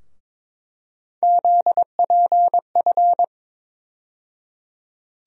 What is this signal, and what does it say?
Morse code, heard as a keyed tone, about 22 wpm, 700 Hz. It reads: ZPF
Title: USA Call Sign Postfixes - 3 Letters 22wpm